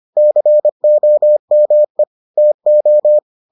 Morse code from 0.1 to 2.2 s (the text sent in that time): COME